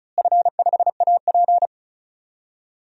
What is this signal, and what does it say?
Morse code, heard as a keyed tone, about 34 wpm, 700 Hz. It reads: F5AP
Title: DXCC-100 Call Signs - 4 Characters - Repeated 3x - 35wpm